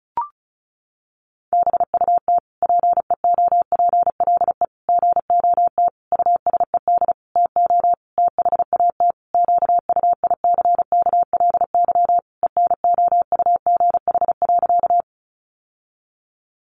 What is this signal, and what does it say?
Morse code, heard as a keyed tone, about 34 wpm, 700 Hz. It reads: BUT PEOPLE GOT USED TO THAT QUICKLY ENOUGH.
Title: Sentences from Top 700 Words 35wpm